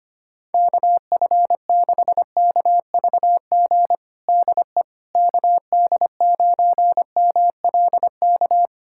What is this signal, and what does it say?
Morse code, heard as a keyed tone, about 25 wpm, 700 Hz. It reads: KF6KVG DE KD9MLK
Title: USA Callsign DE Callsign 25wpm